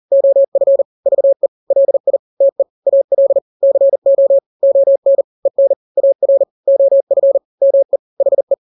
Morse code, heard as a keyed tone, about 40 wpm, 550 Hz. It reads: OF VE LI TE AL CO ON EN AR OF ME HE
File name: n-grams / 2-grams 40 35 WPM